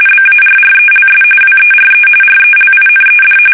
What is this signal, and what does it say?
RTTY, about 45.45 baud. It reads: BARRELS WERE RELINED
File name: fsk850 / battleshipnj